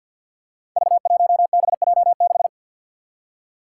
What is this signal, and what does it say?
Morse code, heard as a keyed tone, about 47 wpm, 700 Hz. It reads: V1BJ6
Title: Five Challenging Letter-Number Combinations 50wpm